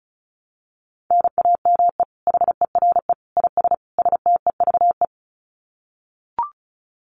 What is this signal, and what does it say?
Morse code, heard as a keyed tone, about 34 wpm, 700 Hz. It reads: NAME HERE IS STEVE
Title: Common QSO Phrases 35wpm